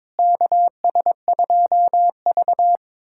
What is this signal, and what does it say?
Morse code, heard as a keyed tone, about 22 wpm, 700 Hz. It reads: KS2V
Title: USA Callsign DE Callsign - 4 Characters - 22wpm